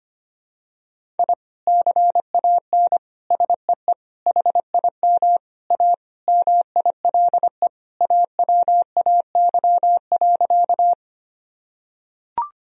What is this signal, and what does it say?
Morse code, heard as a keyed tone, about 25 wpm, 700 Hz. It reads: I CAN SEE HIM A MILE AWAY.
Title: Sentences from Top 1000 Words - Review Part 1 - 25wpm